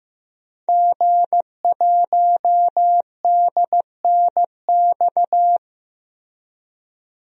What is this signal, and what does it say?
Morse code, heard as a keyed tone, about 15 wpm, 700 Hz. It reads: G1DNX
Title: DXCC-100 Call Signs - 5 Characters - Repeated 3x - 15wpm